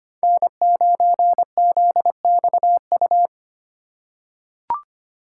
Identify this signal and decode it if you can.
Morse code, heard as a keyed tone, about 25 wpm, 700 Hz. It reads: N9ZXU
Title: USA Call Signs - 2x Character Spacing - 25wpm